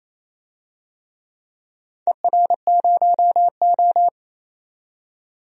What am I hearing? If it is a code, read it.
Morse code, heard as a keyed tone, about 28 wpm, 700 Hz. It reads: ER0O
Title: DXCC-100 Call Signs - 4 Characters - 28wpm